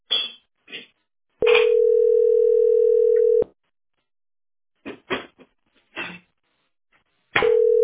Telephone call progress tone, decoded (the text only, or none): ringback tone